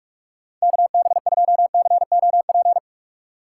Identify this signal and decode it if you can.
Morse code, heard as a keyed tone, about 44 wpm, 700 Hz. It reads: KB2COP